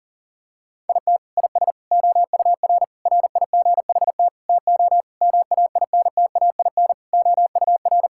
Morse code, heard as a keyed tone, about 40 wpm, 700 Hz. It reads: IT IS OUR RIGHT TO MAINTAIN OUR
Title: Sentences from Top 1000 Words - Review Part 5 - 40wpm